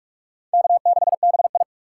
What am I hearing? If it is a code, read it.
Morse code, heard as a keyed tone, about 42 wpm, 700 Hz. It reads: K6BI
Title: USA Call Signs - 3x Character Spacing - 45wpm